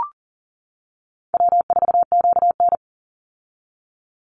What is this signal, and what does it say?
Morse code, heard as a keyed tone, about 40 wpm, 700 Hz. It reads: W4QN
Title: Easy Call Signs - 40wpm